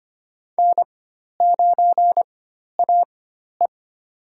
Morse code, heard as a keyed tone, about 25 wpm, spaced as for 12 wpm, 700 Hz. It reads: N9AE
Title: USA Call Signs - 4 Characters - 4x Character Spacing - 25wpm